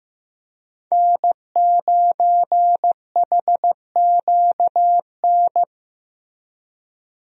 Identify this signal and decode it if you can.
Morse code, heard as a keyed tone, about 16 wpm, 700 Hz. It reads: N9HQN
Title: USA Call Signs - 5 Characters - 15wpm